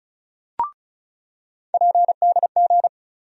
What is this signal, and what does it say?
Morse code, heard as a keyed tone, about 36 wpm, 700 Hz. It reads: PDG
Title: USA Call Sign Postfixes - 3 Letters 35wpm